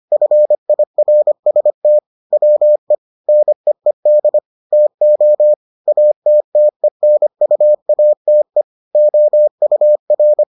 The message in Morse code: FIRST WE NEED TO ATTENUATE OUR

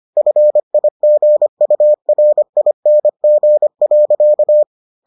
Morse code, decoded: FIGURING.

